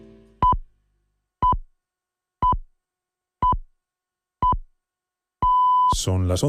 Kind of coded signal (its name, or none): radio time signal